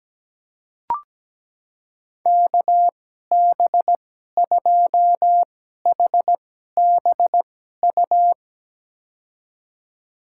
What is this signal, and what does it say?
Morse code, heard as a keyed tone, about 17 wpm, spaced as for 12 wpm, 700 Hz. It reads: KB2HBU
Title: USA Call Signs - 6 Characters - 2x Character Spacing - 17wpm